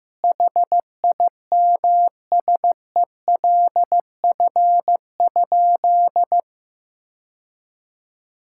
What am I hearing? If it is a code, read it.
Morse code, heard as a keyed tone, about 15 wpm, 700 Hz. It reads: HIMSELF?
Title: Sentences from Top 1600 Words 15wpm